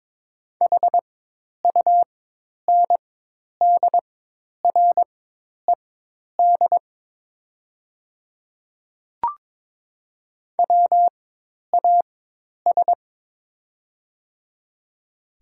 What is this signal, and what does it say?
Morse code, heard as a keyed tone, about 23 wpm, spaced as for 10 wpm, 700 Hz. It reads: HUNDRED WAS
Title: 4x Character Spacing - Top 500 Words - 22wpm